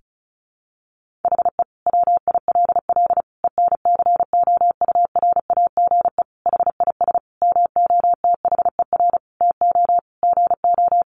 Morse code, heard as a keyed tone, about 34 wpm, 700 Hz. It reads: HE WILL ENCOURAGE HIS MOTHER TO GO
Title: Sentences from Top 1400 Words 35wpm